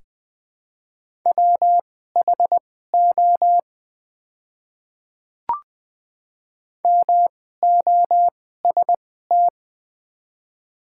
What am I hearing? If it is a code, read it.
Morse code, heard as a keyed tone, about 20 wpm, spaced as for 15 wpm, 700 Hz. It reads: WHO MOST
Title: 2x Character Spacing - Top 100 Words - 20wpm